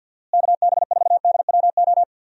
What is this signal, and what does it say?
Morse code, heard as a keyed tone, about 52 wpm, 700 Hz. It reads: KB4DWO
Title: USA Call Signs - 6 Characters - Repeated 2x - 50wpm